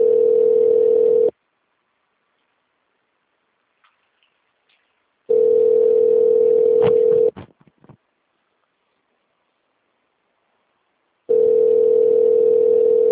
Ringback tone (North America), 440 + 480 Hz.